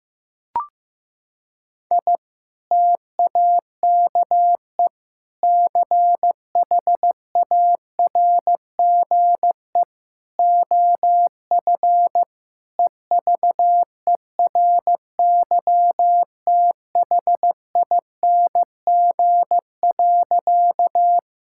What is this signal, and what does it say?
Morse code, heard as a keyed tone, about 16 wpm, 700 Hz. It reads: I TAKE CHARGE OF EVERYTHING.
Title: Sentences from Top 1000 Words - Review Part 6 - 15wpm